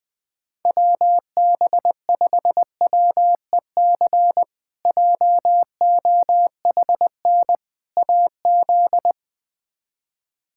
Morse code, heard as a keyed tone, about 21 wpm, 700 Hz. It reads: WB5WEC JOHN AZ
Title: K1USN SST QSO Practice - 20wpm